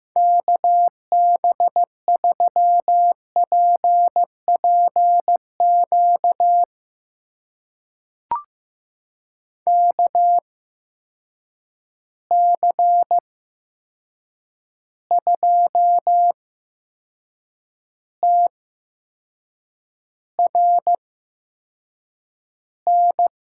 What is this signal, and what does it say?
Morse code, heard as a keyed tone, about 15 wpm, spaced as for 6 wpm, 700 Hz. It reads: KB3PPQ KC2TRN
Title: USA Call Signs - 6 Characters - 8x Character Spacing - 15wpm